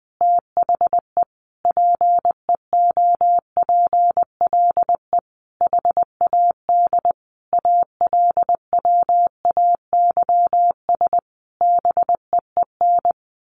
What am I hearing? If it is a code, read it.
Morse code, heard as a keyed tone, about 20 wpm, 700 Hz. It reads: THE PEOPLE HAD ALWAYS BEEN
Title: Sentences from Top 1400 Words 20wpm